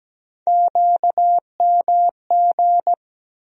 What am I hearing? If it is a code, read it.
Morse code, heard as a keyed tone, about 17 wpm, 700 Hz. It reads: QMG
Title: USA Call Sign Postfixes - 3 Letters 17wpm